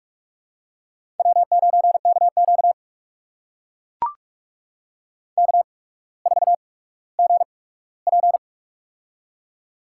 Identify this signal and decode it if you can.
Morse code, heard as a keyed tone, about 44 wpm, spaced as for 19 wpm, 700 Hz. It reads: W9KQ K4GP
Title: USA Call Signs - 4 Characters - 8x Character Spacing - 45wpm